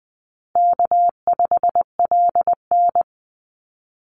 Morse code, heard as a keyed tone, about 20 wpm, 700 Hz. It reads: K5LN
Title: Easy Call Signs Repeated 3x - 20wpm